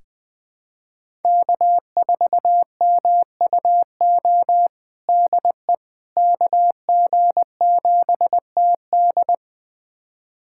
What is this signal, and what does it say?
Morse code, heard as a keyed tone, about 20 wpm, 700 Hz. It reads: K4MUO DE KG7TD
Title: USA Callsign DE Callsign - 5 Characters - 20wpm